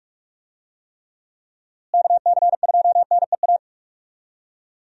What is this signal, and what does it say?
Morse code, heard as a keyed tone, about 46 wpm, 700 Hz. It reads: KC2NEA
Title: USA Call Signs - 6 Characters - Repeated 2x - 45wpm